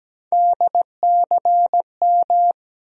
Morse code, heard as a keyed tone, about 17 wpm, 700 Hz. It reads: DCM
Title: USA Call Sign Postfixes - 3 Letters 17wpm